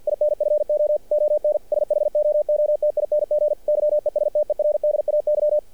Morse code, heard as a keyed tone, about 48 wpm, 600 Hz. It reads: INFO ON SHOOTING YESTERDAY
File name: archived / CRSnet